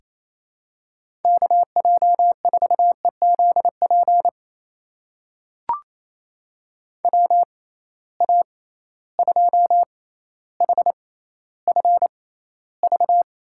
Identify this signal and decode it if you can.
Morse code, heard as a keyed tone, about 28 wpm, spaced as for 14 wpm, 700 Hz. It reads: KJ4EZP WA2HFV
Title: USA Call Signs - 6 Characters - 6x Character Spacing - 28wpm